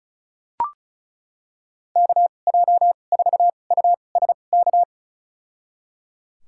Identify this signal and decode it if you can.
Morse code, heard as a keyed tone, about 36 wpm, spaced as for 25 wpm, 700 Hz. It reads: KJ4USK
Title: USA Call Signs - 6 Characters - 2x Character Spacing - 35wpm